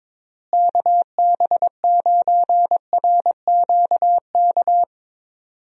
Morse code, heard as a keyed tone, about 22 wpm, 700 Hz. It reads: KB9RQK
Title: USA Call Signs - 6 Characters - Repeated 2x - 22wpm